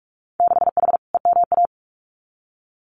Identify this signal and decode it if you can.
Morse code, heard as a keyed tone, about 46 wpm, 690 Hz. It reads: 6H EMA
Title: Winter Field Day Exchanges Encore 45wpm